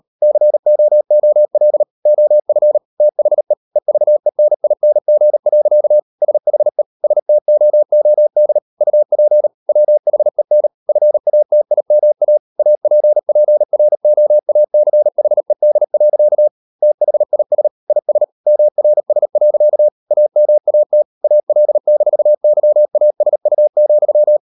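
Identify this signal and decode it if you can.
Morse code, heard as a keyed tone, about 38 wpm, 600 Hz. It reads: COOL OF THE EVENING. SHE STOOD UP WHEN FATIMA APPROACHED. THIS IS MRS. AMAT AL-YASU,
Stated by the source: Morse code practice at 38 WPM, sending Ken Follett's novel Never in English, from CW med Gnister.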